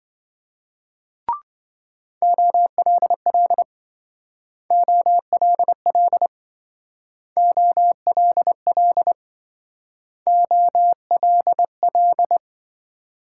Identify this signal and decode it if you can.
Morse code, heard as a keyed tone, about 26 wpm, 700 Hz. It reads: OLL OLL OLL OLL